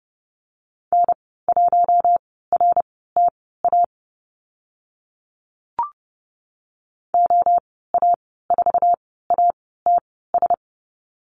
Morse code, heard as a keyed tone, about 30 wpm, spaced as for 17 wpm, 700 Hz. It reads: N1RTA OA4ATS
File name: Lesson 15g - Number 5 - Call Signs 30wpm